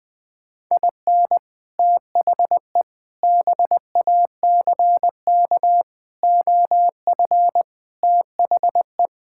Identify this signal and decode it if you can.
Morse code, heard as a keyed tone, about 20 wpm, 700 Hz. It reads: IN THE BACK OF THE
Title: Build A Phrase - 5-Word Phrases - 20wpm